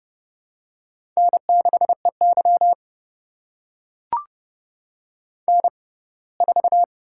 Morse code, heard as a keyed tone, about 30 wpm, spaced as for 20 wpm, 700 Hz. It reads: N6EY N4
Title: USA Call Signs - 6x Character Spacing - 30wpm